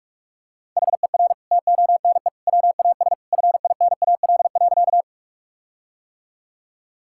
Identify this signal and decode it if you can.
Morse code, heard as a keyed tone, about 46 wpm, 700 Hz. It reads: HER TONE WAS FINAL.
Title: Sentences from Top 1600 Words 45wpm